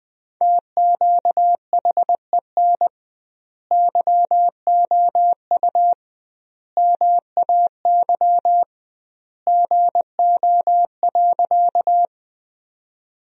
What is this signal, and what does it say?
Morse code, heard as a keyed tone, about 21 wpm, 700 Hz. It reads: TQHEN YOU MAY GO.
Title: 2x Word-Spacing - Sentences from Top 100 Words - 20wpm